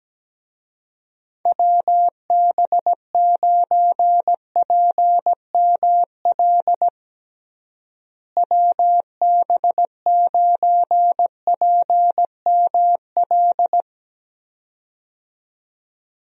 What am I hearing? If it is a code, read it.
Morse code, heard as a keyed tone, about 17 wpm, 700 Hz. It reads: WB9PML WB9PML
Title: USA Call Signs - 6 Characters - Repeated 2x - 17wpm